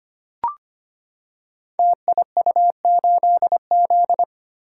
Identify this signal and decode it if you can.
Morse code, heard as a keyed tone, about 25 wpm, 700 Hz. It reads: TIU8Z
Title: Code Groups - 5 Letter:Numbers - 25wpm